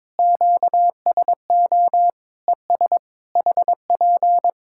Morse code, heard as a keyed tone, about 22 wpm, 700 Hz. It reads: QSO ES HP
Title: Practice QSO - Exchange 4 - Ending - 22wpm